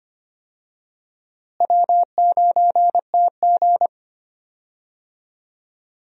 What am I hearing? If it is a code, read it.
Morse code, heard as a keyed tone, about 25 wpm, 700 Hz. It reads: W9TG
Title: USA Call Signs - 4 Characters - Repeated 2x - 25wpm